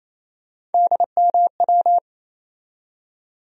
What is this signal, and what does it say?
Morse code, heard as a keyed tone, about 28 wpm, 700 Hz. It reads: DMW
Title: USA Call Sign Postfixes - 3 Letters 28wpm